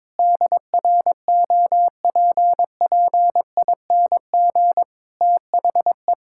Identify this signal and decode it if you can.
Morse code, heard as a keyed tone, about 22 wpm, 700 Hz. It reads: DROPPING THE
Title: Pangram Challenge v2 - 22wpm